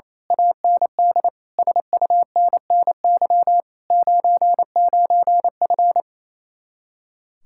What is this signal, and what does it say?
Morse code, heard as a keyed tone, about 26 wpm, 700 Hz. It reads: AND SUNNY 99F
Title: Speed-Racing Common QSO Phrases - 28wpm